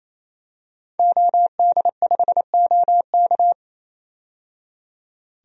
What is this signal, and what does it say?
Morse code, heard as a keyed tone, about 28 wpm, 700 Hz. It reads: OD5OK
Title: DXCC-100 Call Signs - 5 Characters - 28wpm